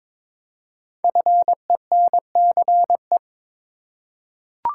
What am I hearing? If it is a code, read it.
Morse code, heard as a keyed tone, about 22 wpm, 700 Hz. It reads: FENCE
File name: 5 Letter Words - First 3 Letters Given - 22wpm